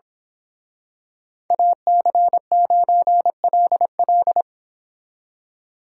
Morse code, heard as a keyed tone, about 26 wpm, 700 Hz. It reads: AC9LL
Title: Speed Racing - USA 2x2 Call Signs 22wpm